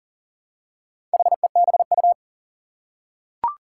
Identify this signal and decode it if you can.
Morse code, heard as a keyed tone, about 40 wpm, 700 Hz. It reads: HEBU
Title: Four Challenging Letter-Number Combinations 40wpm